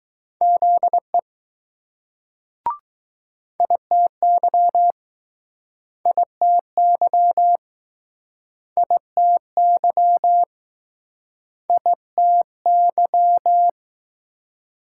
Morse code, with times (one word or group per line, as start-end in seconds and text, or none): 0.4-1.2 ZE
3.6-4.9 ITY
6.1-7.6 ITY
8.8-10.4 ITY
11.7-13.7 ITY